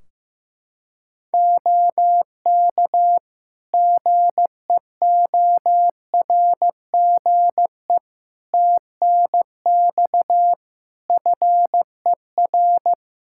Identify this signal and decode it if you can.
Morse code, heard as a keyed tone, about 16 wpm, 700 Hz. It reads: OK GEORGE TNX FER
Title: K1USN SST QSO Practice 15wpm